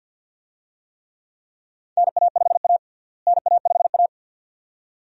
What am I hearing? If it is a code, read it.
Morse code, heard as a keyed tone, about 52 wpm, 700 Hz. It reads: NA5A NA5A
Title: USA Call Signs - 4 Characters - Repeated 2x - 50wpm